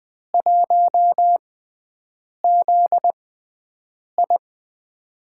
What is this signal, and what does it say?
Morse code, heard as a keyed tone, about 21 wpm, 700 Hz. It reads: 1 Z I
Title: USA Call Signs - 6x Character Spacing - 20wpm